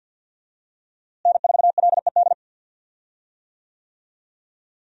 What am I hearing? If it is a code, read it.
Morse code, heard as a keyed tone, about 46 wpm, 700 Hz. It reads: N4LED
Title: USA Call Signs - 5 Characters - Repeated 2x - 50wpm